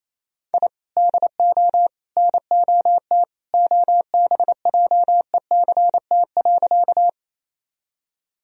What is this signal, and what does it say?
Morse code, heard as a keyed tone, about 28 wpm, 700 Hz. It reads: I DO NOT OBJECT.
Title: Sentences from Top 1000 Words - Review Part 3 - 28wpm